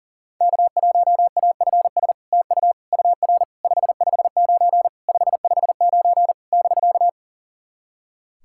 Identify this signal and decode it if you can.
Morse code, heard as a keyed tone, about 40 wpm, 700 Hz. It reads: K1AFS TU UR 559 559 <BK>
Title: SOTA QSO Practice 40wpm